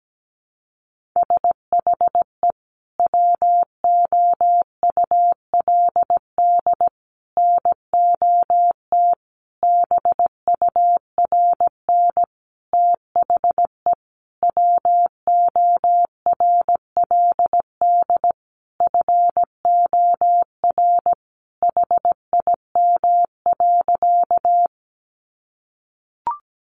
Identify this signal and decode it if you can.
Morse code, heard as a keyed tone, about 17 wpm, 700 Hz. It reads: SHE WOULD NOT BURN THE WORLD FOR HIM.
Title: Sentences from Top 1300 Words 17wpm